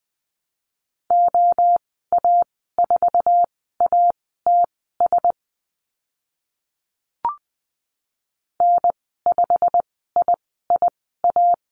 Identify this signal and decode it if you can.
Morse code, heard as a keyed tone, about 20 wpm, spaced as for 14 wpm, 700 Hz. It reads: OA4ATS N5IIA
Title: 088 Lesson 15g - Number 5 - Call Signs 20wpm